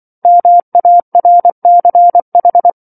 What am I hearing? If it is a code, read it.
Morse code, heard as a keyed tone, about 24 wpm, 700 Hz. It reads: MARCH
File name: 07-Chapter-07 24 WPM